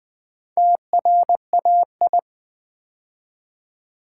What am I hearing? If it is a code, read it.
Morse code, heard as a keyed tone, about 20 wpm, 700 Hz. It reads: TRAI
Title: Build A Word - 2 Letters At A Time - 8-Letter Words - 20wpm